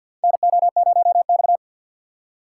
Morse code, heard as a keyed tone, about 50 wpm, 700 Hz. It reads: NO0X
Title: USA Call Signs - 4 Characters - 3x Character Spacing - 50wpm